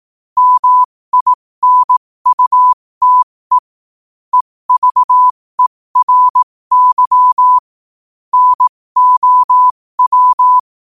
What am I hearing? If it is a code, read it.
Morse code, heard as a keyed tone, about 18 wpm, spaced as for 15 wpm, 1000 Hz. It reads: MINUTE EVERY NOW
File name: qotdfiles / 15wpm